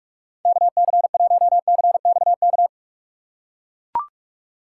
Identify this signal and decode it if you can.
Morse code, heard as a keyed tone, about 44 wpm, 700 Hz. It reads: KC1CXK
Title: USA Call Signs - 6 Characters - 4x Character Spacing - 45wpm